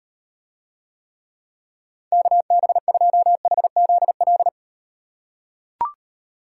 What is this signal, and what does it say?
Morse code, heard as a keyed tone, about 38 wpm, 700 Hz. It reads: KB2HZL